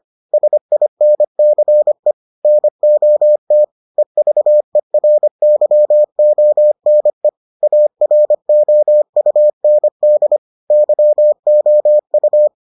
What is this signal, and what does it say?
Morse code, heard as a keyed tone, about 25 wpm, 600 Hz. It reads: SINCE NOT EVERYONE AROUND YOU